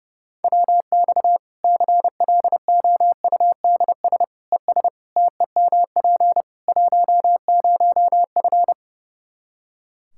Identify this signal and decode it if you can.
Morse code, heard as a keyed tone, about 30 wpm, 700 Hz. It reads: WX CLOUDS ES TEMP 10F
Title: Practice QSO - Exchange 2 - Info - 30wpm